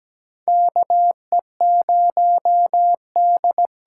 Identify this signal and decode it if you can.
Morse code, heard as a keyed tone, about 17 wpm, 700 Hz. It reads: KE0D
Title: USA Call Signs - 6 Characters - 8x Character Spacing - 17wpm